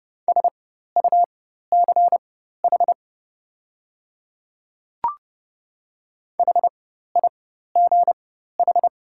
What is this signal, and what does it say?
Morse code, heard as a keyed tone, about 30 wpm, spaced as for 14 wpm, 690 Hz. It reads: SUCH HIGH
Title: 4x Character Spacing - Top 200 Words - 30wpm